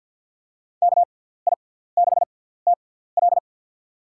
Morse code, heard as a keyed tone, about 52 wpm, spaced as for 17 wpm, 700 Hz. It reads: KI6TL
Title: USA Call Signs - 5 Characters - 6x Character Spacing - 50wpm